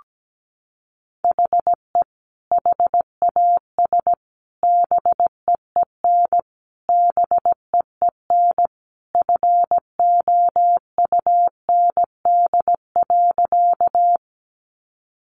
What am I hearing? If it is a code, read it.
Morse code, heard as a keyed tone, about 17 wpm, 700 Hz. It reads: HE HAS BEEN BEEN FOUND.
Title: Sentences from Top 300 Words 17wpm